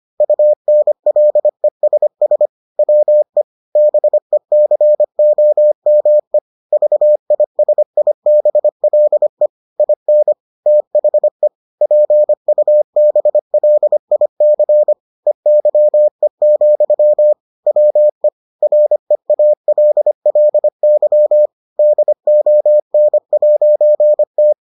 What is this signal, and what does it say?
Morse code, heard as a keyed tone, about 25 wpm, 600 Hz. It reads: UNLESS WE BECOME VISIBLE IN THE PUBLIC EYE, WE REALLY DON'T